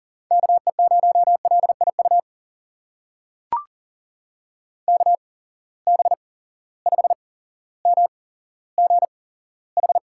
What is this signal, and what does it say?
Morse code, heard as a keyed tone, about 40 wpm, spaced as for 17 wpm, 700 Hz. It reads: KE0LIU KB5MGH
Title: USA Call Signs - 6 Characters - 8x Character Spacing - 40wpm